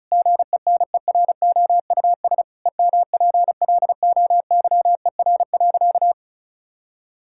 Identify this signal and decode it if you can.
Morse code, heard as a keyed tone, about 36 wpm, 700 Hz. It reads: GENEROUS EMPLOYER.
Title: Pangram Challenge v2 - 35wpm